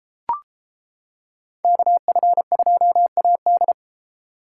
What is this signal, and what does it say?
Morse code, heard as a keyed tone, about 30 wpm, 700 Hz. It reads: KF2AD